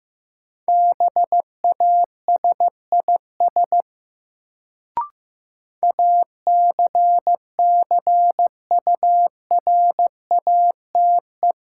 Morse code, heard as a keyed tone, about 15 wpm, 700 Hz. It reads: BASIS ACCURATE
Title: Sentences from Top 1200 Words - New Word Drill Alt - 15wpm